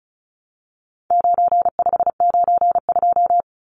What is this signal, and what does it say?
Morse code, heard as a keyed tone, about 34 wpm, 700 Hz. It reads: 9592
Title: Numbers - 4-Digit 35wpm